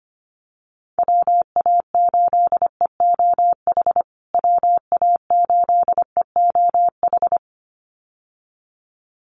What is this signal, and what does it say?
Morse code, heard as a keyed tone, about 25 wpm, 700 Hz. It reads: WA8EOH WA8EOH
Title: USA Call Signs Repeated 2x 25wpm